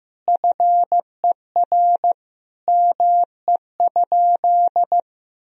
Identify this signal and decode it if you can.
Morse code, heard as a keyed tone, about 15 wpm, 700 Hz. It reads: FER ME?
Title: Sentences from Top 200 Words and Abbreviations 15wpm